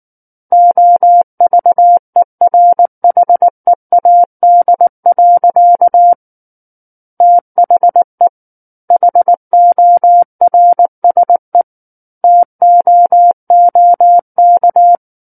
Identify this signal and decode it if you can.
Morse code, heard as a keyed tone, about 19 wpm, 700 Hz. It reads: OVERHEAD. THE HORSE TOOK
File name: Book One / 10-Chapter 10 - 19 WPM